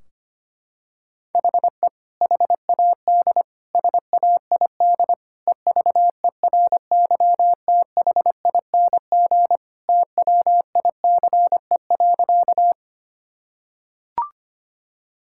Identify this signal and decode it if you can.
Morse code, heard as a keyed tone, about 25 wpm, 700 Hz. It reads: HE HAD SAID EVERYTHING TWICE.
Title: Sentences from Top 1600 Words 25wpm